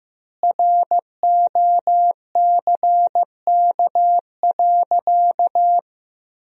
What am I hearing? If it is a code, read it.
Morse code, heard as a keyed tone, about 15 wpm, 700 Hz. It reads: ROCK.
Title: 3x Word-Spacing - Sentences from Top 400 Words - 15wpm